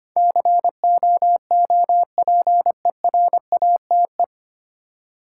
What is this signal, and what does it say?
Morse code, heard as a keyed tone, about 25 wpm, 700 Hz. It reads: COOPERATE
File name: Nine Letter Words - 25wpm